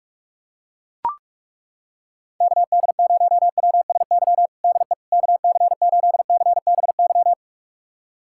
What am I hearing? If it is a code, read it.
Morse code, heard as a keyed tone, about 41 wpm, 700 Hz. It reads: KD0WSY DE KC8CBY
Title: USA Callsign DE Callsign 45wpm